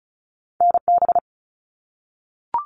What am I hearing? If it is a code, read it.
Morse code, heard as a keyed tone, about 36 wpm, 700 Hz. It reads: NB
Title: Canadian Provinces and Territories 35wpm